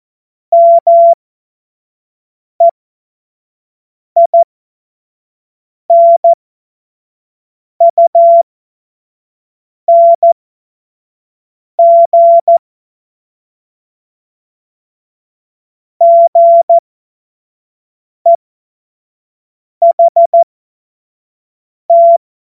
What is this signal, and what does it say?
Morse code, heard as a keyed tone, about 14 wpm, spaced as for 5 wpm, 680 Hz. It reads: MEINUNG GEHT
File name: Morsekurs / Witze / Witz 31. Januar 2023 25 F70